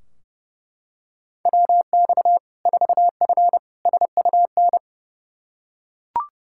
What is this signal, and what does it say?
Morse code, heard as a keyed tone, about 32 wpm, 700 Hz. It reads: WX 4F SUN
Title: Practice QSO - Exchange 2 - Info - 30wpm